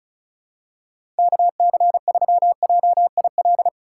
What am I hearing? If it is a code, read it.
Morse code, heard as a keyed tone, about 34 wpm, 700 Hz. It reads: KC3JIL
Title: USA Call Signs - 6 Characters - 8x Character Spacing - 35wpm